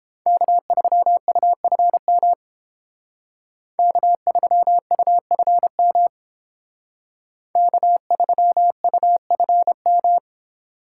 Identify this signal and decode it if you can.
Morse code, heard as a keyed tone, about 30 wpm, 700 Hz. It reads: K3UFM K3UFM K3UFM